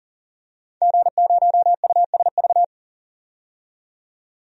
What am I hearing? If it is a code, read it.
Morse code, heard as a keyed tone, about 40 wpm, 700 Hz. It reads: G0USV
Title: DXCC-100 Call Signs - 5 Characters - 40wpm